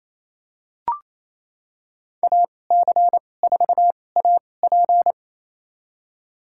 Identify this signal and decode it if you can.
Morse code, heard as a keyed tone, about 28 wpm, spaced as for 20 wpm, 700 Hz. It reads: AC4AP
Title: USA Call Signs - 5 Characters - 2x Character Spacing - 28wpm